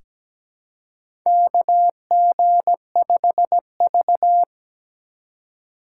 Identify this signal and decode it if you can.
Morse code, heard as a keyed tone, about 17 wpm, 700 Hz. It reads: KG5V
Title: K1USN SST QSO Practice - 17wpm